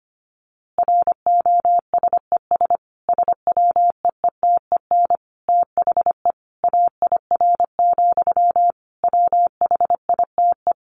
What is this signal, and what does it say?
Morse code, heard as a keyed tone, about 25 wpm, 700 Hz. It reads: ROSES SWEETEN THE AIR, WHITE
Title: A Sound Of Thunder 25wpm